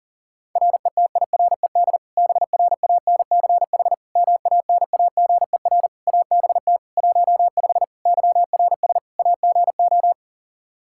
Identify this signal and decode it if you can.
Morse code, heard as a keyed tone, about 40 wpm, 700 Hz. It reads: RETIRED BRANCH MANAGER ABT 15 YRS AGO
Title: Practice QSO - Exchange 3 - More Info - 40wpm